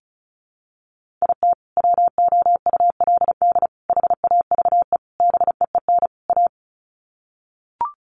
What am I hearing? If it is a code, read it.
Morse code, heard as a keyed tone, about 34 wpm, 700 Hz. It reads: IT WOULD HAVE BEEN A
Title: Sets of 5 Words Encore 35wpm